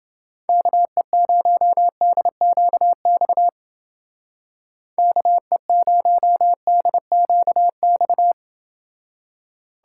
Morse code, heard as a keyed tone, about 29 wpm, 700 Hz. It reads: KE0DQX KE0DQX